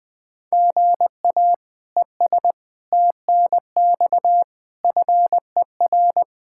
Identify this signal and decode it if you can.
Morse code, heard as a keyed tone, about 20 wpm, 700 Hz. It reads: GA ES TNX FER
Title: Practice QSO - Exchange 1 - Beginning - 20wpm